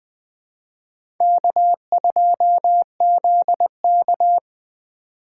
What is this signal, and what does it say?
Morse code, heard as a keyed tone, about 21 wpm, 700 Hz. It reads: K2ZK
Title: USA Call Signs - 4 Characters - 2x Character Spacing - 20wpm